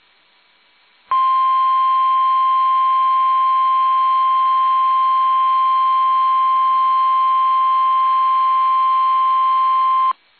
Emergency Alert System (SAME) tones (the 1050 Hz attention tone). It is heard from 1.1 to 10.2 s.
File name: eas / KUSO / KUSO